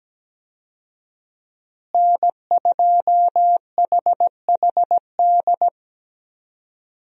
Morse code, heard as a keyed tone, about 17 wpm, 700 Hz. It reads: N2HHD
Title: USA Call Signs - 5 Characters - Repeated 2x - 17wpm